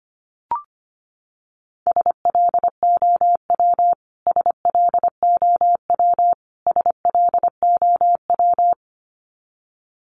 Morse code, heard as a keyed tone, about 25 wpm, 700 Hz. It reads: SLOW SLOW SLOW
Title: Top 500 Words Repeated 25wpm